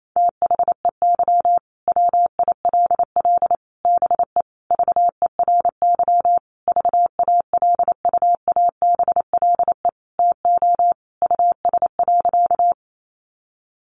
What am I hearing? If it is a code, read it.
Morse code, heard as a keyed tone, about 28 wpm, 700 Hz. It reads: THEY WILL BE VERY VALUABLE TO US.
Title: Sentences from Top 1100 Words 28wpm